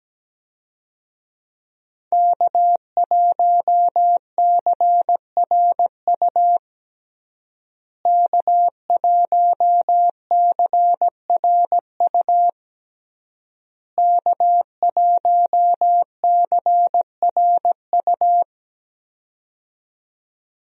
Morse code, heard as a keyed tone, about 17 wpm, 700 Hz. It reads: K1CRU K1CRU K1CRU
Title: USA Call Signs - 5 Characters - Repeated 3x - 17wpm